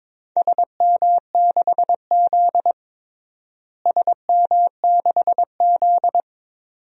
Morse code, heard as a keyed tone, about 22 wpm, 700 Hz. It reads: SM6Z SM6Z
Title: DXCC-100 Call Signs - 4 Characters - Repeated 2x - 22wpm